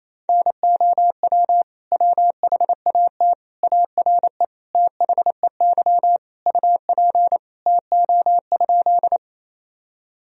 Morse code, heard as a keyed tone, about 28 wpm, 700 Hz. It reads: NOW WHAT ARE THEY UP TO?
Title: Sentences from Top 1000 Words - Review Part 1 - 28wpm